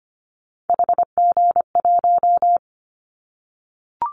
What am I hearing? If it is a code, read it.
Morse code, heard as a keyed tone, about 25 wpm, 700 Hz. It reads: HG1
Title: Callsign Prefixes for Top 50 DXCC - 25wpm